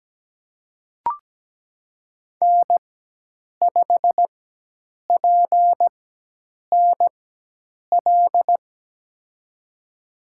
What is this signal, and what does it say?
Morse code, heard as a keyed tone, about 17 wpm, spaced as for 8 wpm, 700 Hz. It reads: N5PNL
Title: USA Call Signs - 4x Character Spacing - 17wpm